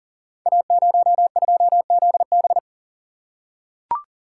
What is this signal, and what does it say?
Morse code, heard as a keyed tone, about 42 wpm, 690 Hz. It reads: A02ZB
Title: Code Groups - 5 Letter:Numbers - 40wpm